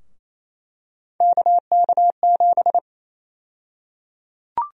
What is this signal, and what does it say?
Morse code, heard as a keyed tone, about 28 wpm, 700 Hz. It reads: KK7
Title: USA Call Sign Prefixes - Repeated 2x - 28wpm